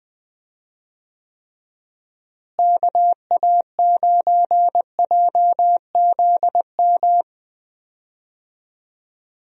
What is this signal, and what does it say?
Morse code, heard as a keyed tone, about 20 wpm, 700 Hz. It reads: KA9JZM
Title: USA Call Signs - 6 Characters - Repeated 2x - 20wpm